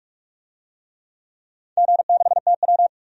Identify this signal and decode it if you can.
Morse code, heard as a keyed tone, about 46 wpm, 700 Hz. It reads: G6TW
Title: DXCC-100 Call Signs - 4 Characters - 45wpm